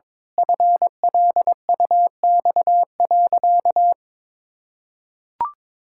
Morse code, heard as a keyed tone, about 22 wpm, 700 Hz. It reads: FLUX.
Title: Pangram Challenge v2 - 22wpm